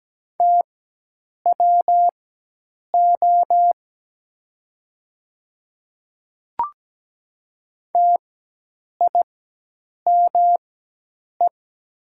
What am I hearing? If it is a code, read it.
Morse code, heard as a keyed tone, about 17 wpm, spaced as for 8 wpm, 700 Hz. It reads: TWO TIME
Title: 4x Character Spacing - Top 100 Words - 17wpm